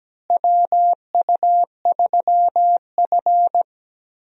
Morse code, heard as a keyed tone, about 17 wpm, 700 Hz. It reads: WU3F
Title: USA Call Signs - 4 Characters - Repeated 3x - 17wpm